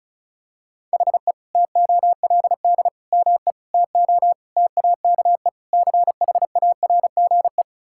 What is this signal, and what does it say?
Morse code, heard as a keyed tone, about 36 wpm, 700 Hz. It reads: HE TOLD ME TO TAKE CHARGE
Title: Sentences from Top 1000 Words - Review Part 6 - 35wpm